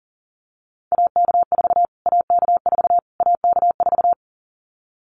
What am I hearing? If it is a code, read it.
Morse code, heard as a keyed tone, about 37 wpm, 700 Hz. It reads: AK4 AK4 AK4
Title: Callsign Prefixes for Top 25 DXCC - Repeated 3x - 40wpm